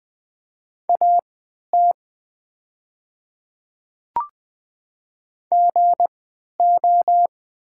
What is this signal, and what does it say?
Morse code, heard as a keyed tone, about 20 wpm, spaced as for 11 wpm, 700 Hz. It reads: AT GO
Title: 3x Character Spacing - Top 100 Words - 20wpm